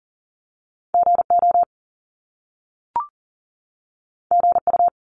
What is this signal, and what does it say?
Morse code, heard as a keyed tone, about 39 wpm, 700 Hz. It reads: GO GU
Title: Top Two Letter Combinations in Words 40wpm